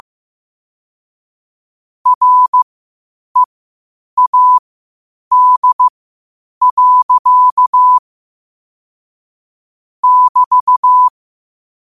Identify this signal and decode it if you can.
Morse code, heard as a keyed tone, about 15 wpm, spaced as for 8 wpm, 1000 Hz. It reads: READ. <BT>
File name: qotdfiles / 8wpm